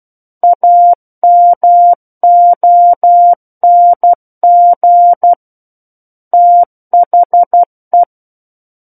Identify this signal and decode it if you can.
Morse code, heard as a keyed tone, about 12 wpm, 700 Hz. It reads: AMONG THE
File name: Book One / 03-Chapter 3 - 12 WPM